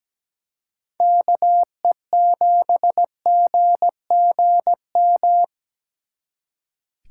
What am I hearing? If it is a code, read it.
Morse code, heard as a keyed tone, about 20 wpm, 700 Hz. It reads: KE7GGM
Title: Speed Racing - USA Call Signs 17wpm